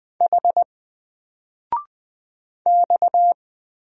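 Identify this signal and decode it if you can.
Morse code, heard as a keyed tone, about 20 wpm, 700 Hz. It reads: H X
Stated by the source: Single letter and number Morse drill at 20 wpm, from N4LCD.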